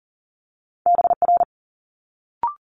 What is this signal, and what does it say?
Morse code, heard as a keyed tone, about 42 wpm, 700 Hz. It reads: BR
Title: Multi-letter Phonemes Quick Fire 40wpm